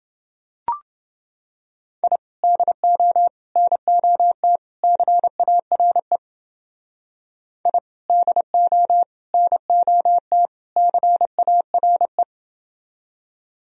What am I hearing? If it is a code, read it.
Morse code, heard as a keyed tone, about 29 wpm, 700 Hz. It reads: I DO NOT CARE I DO NOT CARE